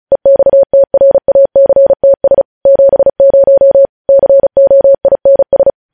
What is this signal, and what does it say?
Morse code, heard as a keyed tone, about 36 wpm, 550 Hz. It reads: EXTRACTS 70 COINS
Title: News Headlines at 35